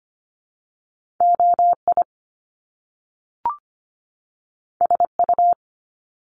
Morse code, heard as a keyed tone, about 25 wpm, 700 Hz. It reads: OI SU
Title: Top Two Letter Combinations in Words 25wpm